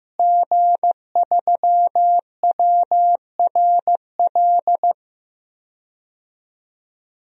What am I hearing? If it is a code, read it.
Morse code, heard as a keyed tone, about 15 wpm, 700 Hz. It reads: G3WRL
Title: DXCC-100 Call Signs - 5 Characters - Repeated 2x - 15wpm